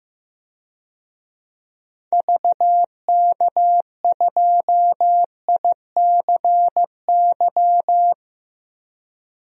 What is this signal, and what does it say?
Morse code, heard as a keyed tone, about 15 wpm, 700 Hz. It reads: VK2ICY
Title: DXCC-100 Call Signs - 6 Characters - 15wpm